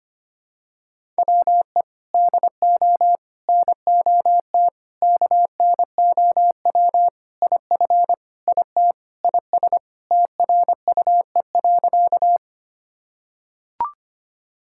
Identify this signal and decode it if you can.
Morse code, heard as a keyed tone, about 25 wpm, 700 Hz. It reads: WE DO NOT KNOW IF IT IS TRUE.
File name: practice / 25wpm / Sentences with 4-Word Phrases - 25wpm